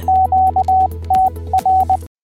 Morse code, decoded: QTR